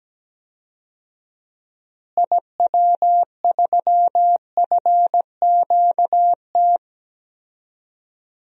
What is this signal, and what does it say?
Morse code, heard as a keyed tone, about 17 wpm, 700 Hz. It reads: IW3FQT
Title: DXCC-100 Call Signs - 6 Characters - Repeated 3x - 17wpm